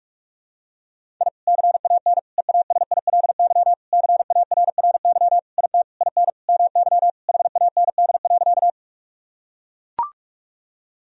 Morse code, heard as a keyed tone, about 41 wpm, 700 Hz. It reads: I CAN EASILY CARRY IT IN MY HAND.
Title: Sentences from Top 1000 Words - Review Part 1 - 45wpm